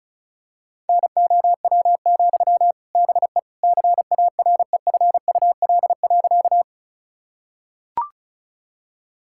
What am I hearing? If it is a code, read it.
Morse code, heard as a keyed tone, about 36 wpm, 700 Hz. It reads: NOW, BE CAREFUL.
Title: Sentences from Top 1600 Words 35wpm